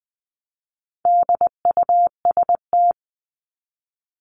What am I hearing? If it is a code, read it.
Morse code, heard as a keyed tone, about 20 wpm, 700 Hz. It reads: DUST
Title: Top 1751-2000 Words 20wpm